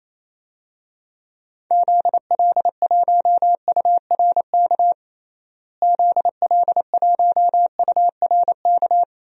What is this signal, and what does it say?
Morse code, heard as a keyed tone, about 28 wpm, 700 Hz. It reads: ZL1URK ZL1URK
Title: DXCC-100 Call Signs - 6 Characters - Repeated 3x - 28wpm